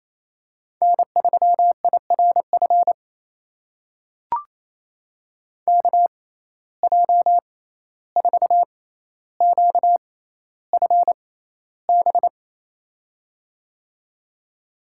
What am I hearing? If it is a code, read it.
Morse code, heard as a keyed tone, about 28 wpm, spaced as for 14 wpm, 700 Hz. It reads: N3IRF KJ4QFB
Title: USA Call Signs - 6x Character Spacing - 28wpm